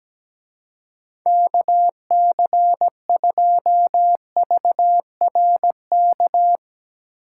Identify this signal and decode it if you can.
Morse code, heard as a keyed tone, about 17 wpm, 700 Hz. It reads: KC2VRK
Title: Speed Racing - USA 2x3 Call Signs 17wpm